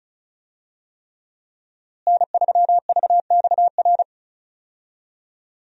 Morse code, heard as a keyed tone, about 36 wpm, 700 Hz. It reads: N3VXR
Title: USA Call Signs - 5 Characters - Repeated 3x - 35wpm